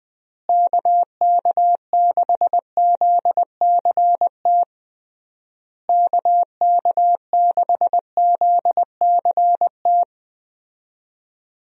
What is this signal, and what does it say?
Morse code, heard as a keyed tone, about 20 wpm, 700 Hz. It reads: KK6ZCT KK6ZCT
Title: USA Call Signs - 6 Characters - Repeated 3x - 20wpm